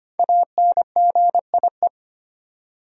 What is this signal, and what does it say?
Morse code, heard as a keyed tone, about 25 wpm, 700 Hz. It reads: ANGIE
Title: USA Common Names - 25wpm